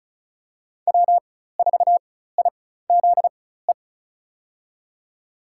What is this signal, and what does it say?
Morse code, heard as a keyed tone, about 36 wpm, spaced as for 16 wpm, 700 Hz. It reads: W4IZE
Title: USA Call Signs - 5 Characters - 4x Character Spacing - 35wpm